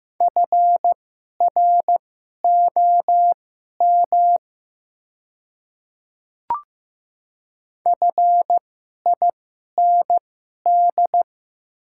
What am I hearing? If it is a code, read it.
Morse code, heard as a keyed tone, about 15 wpm, spaced as for 11 wpm, 700 Hz. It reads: FROM FIND
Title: 2x Character Spacing - Top 100 Words - 15wpm